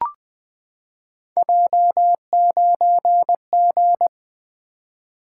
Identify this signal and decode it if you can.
Morse code, heard as a keyed tone, about 20 wpm, 700 Hz. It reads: J9G
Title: Three Letter-Number Combinations - 20wpm